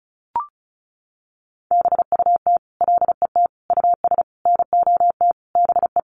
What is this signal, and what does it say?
Morse code, heard as a keyed tone, about 34 wpm, 700 Hz. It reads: BUT LET US NOT BE
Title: Sentences from Top 1500 Words 35wpm